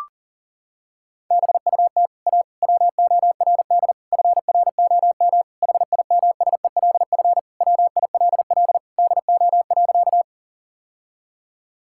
Morse code, heard as a keyed tone, about 40 wpm, 700 Hz. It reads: BUT A WORD FROM HIMSELF WILL DO.
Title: Sentences from Top 1600 Words 40wpm